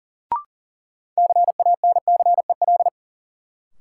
Morse code, heard as a keyed tone, about 40 wpm, 700 Hz. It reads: CANCEL
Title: Sentences from Top 1200 Words - New Word Drill - 40wpm